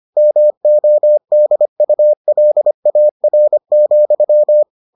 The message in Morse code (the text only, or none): MODULAR,